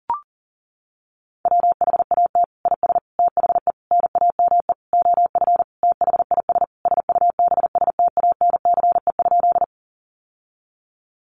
Morse code, heard as a keyed tone, about 40 wpm, 700 Hz. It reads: WHAT IS THE NAME OF THIS SUBSTANCE?
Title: Sentences from Top 1500 Words 40wpm